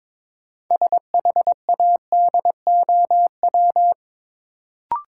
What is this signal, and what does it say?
Morse code, heard as a keyed tone, about 21 wpm, 700 Hz. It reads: SHADOW
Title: Sets of 5 Words Encore - New Word Drill - 22wpm